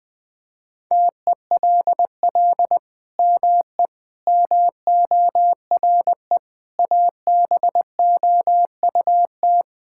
Morse code, heard as a keyed tone, about 21 wpm, 700 Hz. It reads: TELL ME MORE ABOUT
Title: Sentences from Top 1000 Words - Review Part 5 - 20wpm